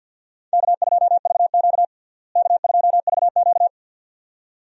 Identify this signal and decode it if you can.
Morse code, heard as a keyed tone, about 53 wpm, 700 Hz. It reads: K2VQ K2VQ
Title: USA Call Signs - 4 Characters - Repeated 2x - 50wpm